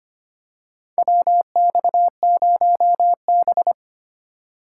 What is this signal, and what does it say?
Morse code, heard as a keyed tone, about 25 wpm, 690 Hz. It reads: WX0B
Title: USA Call Signs - 4 Characters - 4x Character Spacing - 25wpm